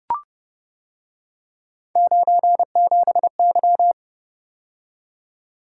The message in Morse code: 97Y

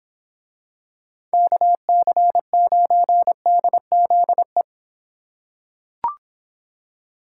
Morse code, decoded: KC9DZE

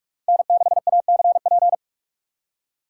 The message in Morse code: N6ACP

